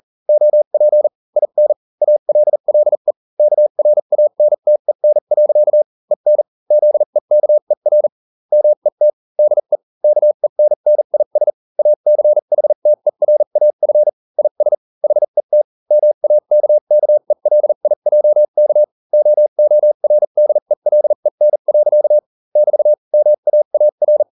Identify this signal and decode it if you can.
Morse code, heard as a keyed tone, about 40 wpm, spaced as for 35 wpm, 600 Hz. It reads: OP IN ALLE KRANTEN. EN ZEKER MET DE KENNIS ACHTERAF IS HET MAKKELIJK OORDELEN. <BT> MAAR